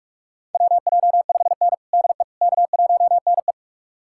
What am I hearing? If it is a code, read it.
Morse code, heard as a keyed tone, about 47 wpm, 700 Hz. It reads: WJ5N DE K1NE
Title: USA Callsign DE Callsign - 4 Characters - 45wpm